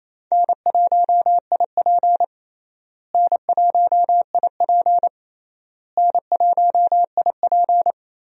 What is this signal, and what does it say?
Morse code, heard as a keyed tone, about 28 wpm, 700 Hz. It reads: N1IP N1IP N1IP
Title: USA Call Signs - 4 Characters - Repeated 3x - 28wpm